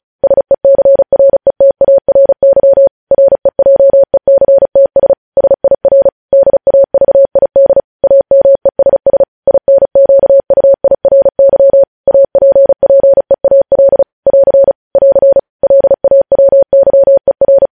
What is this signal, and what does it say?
Morse code, heard as a keyed tone, about 36 wpm, 550 Hz. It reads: SECRETARY REJECTS SIR DAVID AMESS INQUIRY APPEAL <AR> <AR> LAWYER
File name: podcast / News headlines-1x-35x35